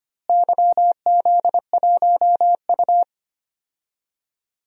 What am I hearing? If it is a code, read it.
Morse code, heard as a keyed tone, about 25 wpm, 700 Hz. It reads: YZ1U
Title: DXCC-100 Call Signs - 4 Characters - Repeated 3x - 25wpm